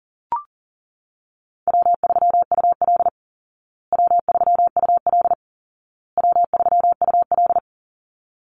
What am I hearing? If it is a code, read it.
Morse code, heard as a keyed tone, about 40 wpm, 700 Hz. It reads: W3UL W3UL W3UL
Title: Easy Call Signs Repeated 3x - 40wpm